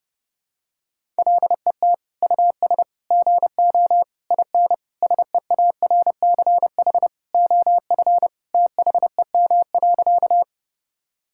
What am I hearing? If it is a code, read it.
Morse code, heard as a keyed tone, about 31 wpm, 700 Hz. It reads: LET US GO IN SEARCH OF THEM.
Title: Sentences from Top 1000 Words - Review Part 5 - 30wpm